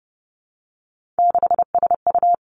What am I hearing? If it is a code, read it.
Morse code, heard as a keyed tone, about 30 wpm, 700 Hz. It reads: 6SU
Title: Three Challenging Letter-Number Combinations 30wpm